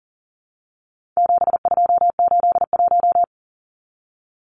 Morse code, heard as a keyed tone, about 40 wpm, 700 Hz. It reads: 7281
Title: Numbers - 4-Digit 40wpm